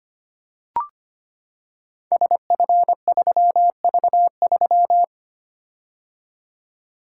Morse code, heard as a keyed tone, about 25 wpm, 700 Hz. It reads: SF3V3
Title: Code Groups - 5 Letter:Numbers - 25wpm